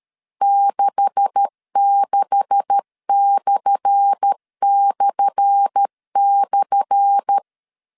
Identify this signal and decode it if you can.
Morse code, heard as a keyed tone, about 13 wpm, 800 Hz. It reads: 66///
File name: LEZIONI / 10 SIGNS / 10A SLASH 12wpm 2314